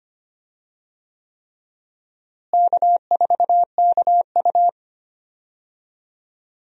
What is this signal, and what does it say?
Morse code, heard as a keyed tone, about 25 wpm, 700 Hz. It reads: K4KU
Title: USA Call Signs - 4 Characters - Repeated 3x - 25wpm